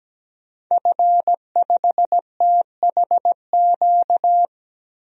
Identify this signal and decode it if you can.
Morse code, heard as a keyed tone, about 17 wpm, 700 Hz. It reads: F5THQ
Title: DXCC-100 Call Signs - 5 Characters - 17wpm